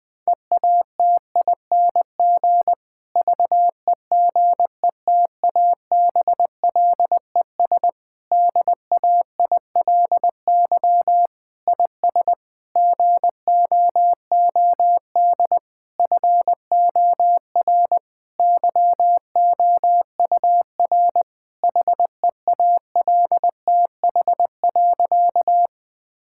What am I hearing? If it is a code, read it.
Morse code, heard as a keyed tone, about 21 wpm, 700 Hz. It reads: EATING VEGETABLES DAILY IS GOOD FOR YOUR HEALTH.
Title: Difficult Sentences with Hints - 20wpm